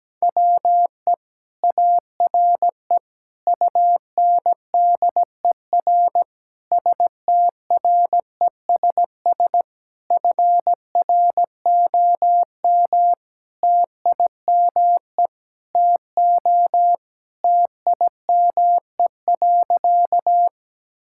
Morse code, heard as a keyed tone, about 17 wpm, 700 Hz. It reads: WE ARE UNDER STRESS FROM TIME TO TIME.
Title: Sentences from Top 1000 Words - Review Part 6 - 17wpm